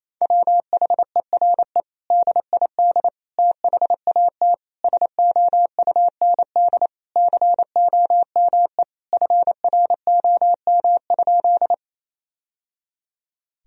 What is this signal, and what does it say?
Morse code, heard as a keyed tone, about 27 wpm, 700 Hz. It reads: WHERE DID THAT SOUND COME FROM?
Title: Sentences from Top 1000 Words - Review Part 1 - 28wpm